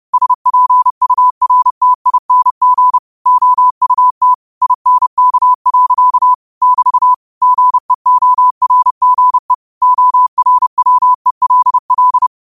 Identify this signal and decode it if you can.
Morse code, heard as a keyed tone, about 30 wpm, 1000 Hz. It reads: SPURTING OUT INK. <BT> GEORGE ORWELL
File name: qotdfiles / 30wpm